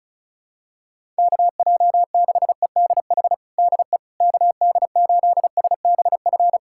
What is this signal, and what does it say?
Morse code, heard as a keyed tone, about 34 wpm, 700 Hz. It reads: KJ6EDH DE KD8SBF
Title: USA Callsign DE Callsign - 6 Characters - 35wpm